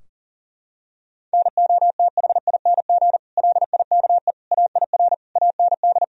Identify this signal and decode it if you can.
Morse code, heard as a keyed tone, about 40 wpm, 700 Hz. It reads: NOTHING LIKE AIR AND
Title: Sentences from Top 1000 Words - Review Part 4 - 40wpm